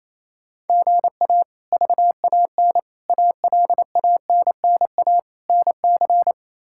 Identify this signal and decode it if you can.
Morse code, heard as a keyed tone, about 28 wpm, 700 Hz. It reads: GA VAN ALANNA NC
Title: K1USN SST QSO Practice - 28wpm